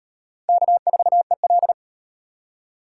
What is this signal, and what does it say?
Morse code, heard as a keyed tone, about 38 wpm, 700 Hz. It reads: K4EL